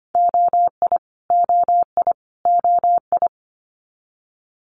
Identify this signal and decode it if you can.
Morse code, heard as a keyed tone, about 25 wpm, 700 Hz. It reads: OI OI OI
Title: Multi-letter Phonemes Repeated 3x 25wpm